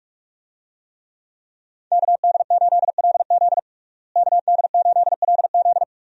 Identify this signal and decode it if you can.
Morse code, heard as a keyed tone, about 41 wpm, 700 Hz. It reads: KD8LZ KD8LZ
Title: USA Call Signs - 5 Characters - Repeated 3x - 45wpm